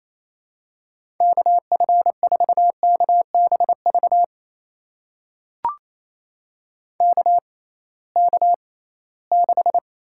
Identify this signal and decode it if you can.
Morse code, heard as a keyed tone, about 28 wpm, spaced as for 18 wpm, 700 Hz. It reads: KF4KBV KK6
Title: USA Call Signs - 6 Characters - 6x Character Spacing - 28wpm